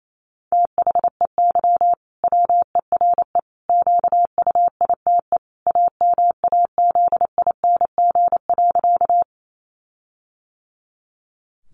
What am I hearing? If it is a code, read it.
Morse code, heard as a keyed tone, about 28 wpm, 700 Hz. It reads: THEY WERE QUITE AMAZING.
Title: Sentences from Top 1500 Words 28wpm